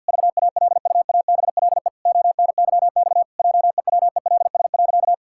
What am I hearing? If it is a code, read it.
Morse code, heard as a keyed tone, about 50 wpm, 700 Hz. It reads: VALUABLE ONYX JEWELS.
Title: Pangram Challenge v2 - 50wpm